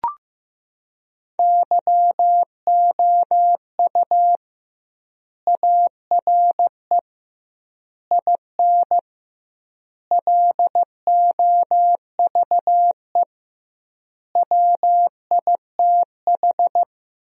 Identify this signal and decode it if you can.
Morse code, heard as a keyed tone, about 15 wpm, 700 Hz. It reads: YOU ARE IN LOVE WITH
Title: 2x Word-Spacing - Sentences from Top 400 Words - 15wpm